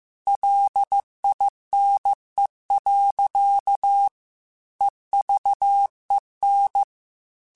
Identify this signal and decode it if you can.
Morse code, heard as a keyed tone, about 15 wpm, 800 Hz. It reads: LINE. EVEN
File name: Technical Subjects / Technical Subjects-mp3s / Sw Pwr Sup @13